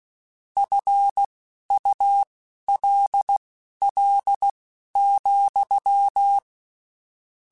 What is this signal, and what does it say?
Morse code, heard as a keyed tone, about 16 wpm, spaced as for 11 wpm, 800 Hz. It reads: FULL,
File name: Spaced / Chuckles Spaced-mp3s / HalfFull sp @15